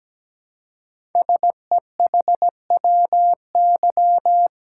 Morse code, heard as a keyed tone, about 17 wpm, 700 Hz. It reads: SEHWY
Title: Code Groups - 5 Letters - 17wpm